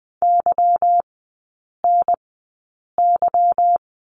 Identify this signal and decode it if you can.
Morse code, heard as a keyed tone, about 20 wpm, spaced as for 8 wpm, 690 Hz. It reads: YNY
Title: 145 Lesson 24e - Letter y - Three Characters 20wpm